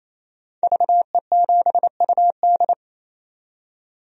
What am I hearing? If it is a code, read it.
Morse code, heard as a keyed tone, about 27 wpm, 700 Hz. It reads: VE7UD
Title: DXCC-100 Call Signs - 5 Characters - 28wpm